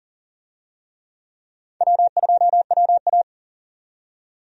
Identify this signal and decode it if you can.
Morse code, heard as a keyed tone, about 40 wpm, 700 Hz. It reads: W2WA
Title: USA Call Signs - 4 Characters - 40wpm